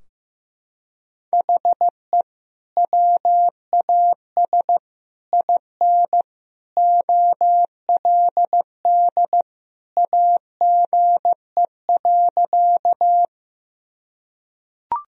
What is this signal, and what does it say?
Morse code, heard as a keyed tone, about 15 wpm, 700 Hz. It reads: HE WAS IN OLD AGE.
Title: Sentences from Top 1000 Words - Review Part 3 - 15wpm